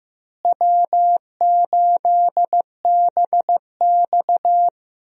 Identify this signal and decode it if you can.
Morse code, heard as a keyed tone, about 15 wpm, 700 Hz. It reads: W8BX
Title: USA Call Signs - 4 Characters - Repeated 3x - 15wpm